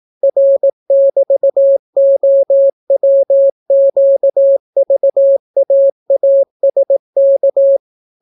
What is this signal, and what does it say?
Morse code, heard as a keyed tone, about 18 wpm, 550 Hz. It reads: R<BT>OWQVAASK